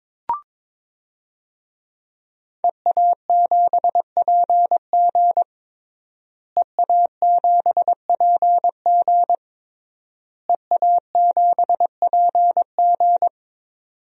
Morse code, heard as a keyed tone, about 22 wpm, 700 Hz. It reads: EA7PG EA7PG EA7PG
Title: DXCC-100 Call Signs - 5 Characters - Repeated 3x - 22wpm